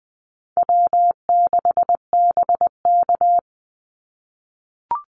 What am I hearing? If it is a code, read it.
Morse code, heard as a keyed tone, about 20 wpm, 700 Hz. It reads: W6BK
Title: Call Signs - 20wpm